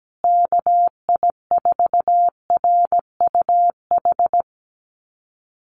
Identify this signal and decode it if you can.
Morse code, heard as a keyed tone, about 17 wpm, 700 Hz. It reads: KI4RUH
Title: USA Call Signs Repeated 2x 17wpm